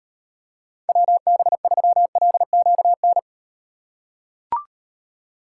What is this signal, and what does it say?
Morse code, heard as a keyed tone, about 38 wpm, 700 Hz. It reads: WB3LQN